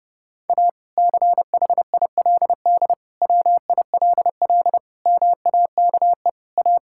Morse code, heard as a keyed tone, about 30 wpm, 700 Hz. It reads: A CHILD WILL MAKE A
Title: Sentences from Top 1000 Words - Review Part 6 - 30wpm